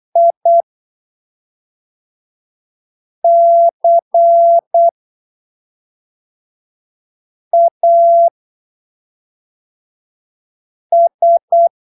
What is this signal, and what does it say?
Morse code, heard as a keyed tone, about 8 wpm, spaced as for 3 wpm, 680 Hz. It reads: ICAS
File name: arquivos / texto / texto 10 08ppm